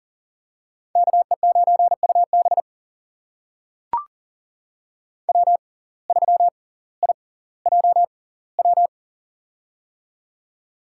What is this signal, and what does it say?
Morse code, heard as a keyed tone, about 40 wpm, spaced as for 20 wpm, 700 Hz. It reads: KE9UB W3IJW
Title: USA Call Signs - 5 Characters - 6x Character Spacing - 40wpm